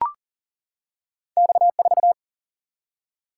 Morse code, heard as a keyed tone, about 40 wpm, 700 Hz. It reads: X 4
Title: Single-Letter Single-Number 40wpm